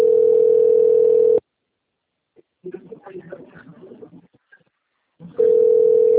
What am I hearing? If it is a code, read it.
Ringback tone (North America), 440 + 480 Hz, with speech.
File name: rg-1600-unknown- - - .19438